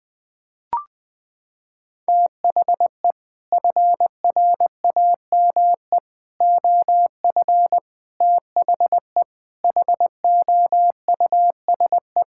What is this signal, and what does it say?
Morse code, heard as a keyed tone, about 21 wpm, 700 Hz. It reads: THE FRAME OF THE HOUSE
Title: Sentences from Top 1000 Words - Review Part 6 - 20wpm